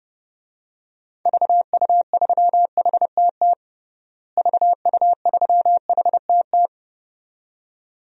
Morse code, heard as a keyed tone, about 30 wpm, 700 Hz. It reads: VU3HTT VU3HTT
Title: DXCC-100 Call Signs - 6 Characters - Repeated 2x - 30wpm